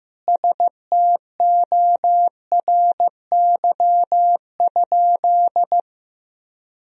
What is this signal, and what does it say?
Morse code, heard as a keyed tone, about 15 wpm, 700 Hz. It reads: STORY?
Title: Sentences from Top 400 Words and Abbreviations 15wpm